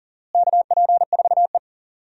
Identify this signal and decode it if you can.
Morse code, heard as a keyed tone, about 40 wpm, 700 Hz. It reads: KP4E
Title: DXCC-100 Call Signs - 4 Characters - Repeated 3x - 40wpm